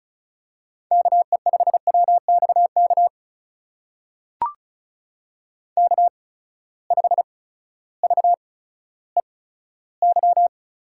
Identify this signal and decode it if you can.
Morse code, heard as a keyed tone, about 36 wpm, spaced as for 16 wpm, 700 Hz. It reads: KE5WXK K5VEY
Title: USA Call Signs - 8x Character Spacing - 35wpm